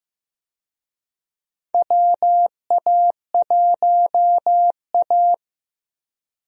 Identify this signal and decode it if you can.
Morse code, heard as a keyed tone, about 15 wpm, 700 Hz. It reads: WA1A
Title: USA Call Signs - 4 Characters - 15wpm